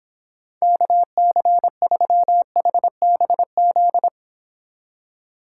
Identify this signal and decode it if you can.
Morse code, heard as a keyed tone, about 29 wpm, 700 Hz. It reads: KC3HBZ